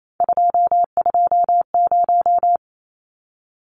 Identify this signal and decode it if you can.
Morse code, heard as a keyed tone, about 28 wpm, 700 Hz. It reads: 220
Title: Numbers - 3-Digit 28wpm